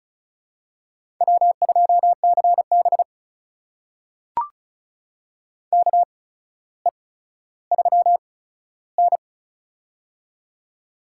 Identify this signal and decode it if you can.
Morse code, heard as a keyed tone, about 34 wpm, spaced as for 15 wpm, 700 Hz. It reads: W2CB KE3N
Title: USA Call Signs - 4 Characters - 8x Character Spacing - 35wpm